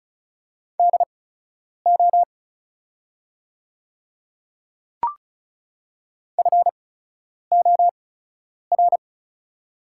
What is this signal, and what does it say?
Morse code, heard as a keyed tone, about 36 wpm, spaced as for 10 wpm, 700 Hz. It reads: DO FOR
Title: 8x Character Spacing - Top 100 Words - 35wpm